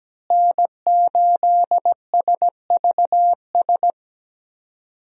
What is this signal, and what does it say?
Morse code, heard as a keyed tone, about 17 wpm, 700 Hz. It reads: N8SVS
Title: USA Call Signs - 5 Characters - Repeated 2x - 17wpm